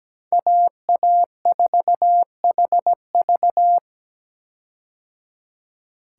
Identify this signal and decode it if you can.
Morse code, heard as a keyed tone, about 17 wpm, 700 Hz. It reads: AA4HV
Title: K1USN SST QSO Practice - 17wpm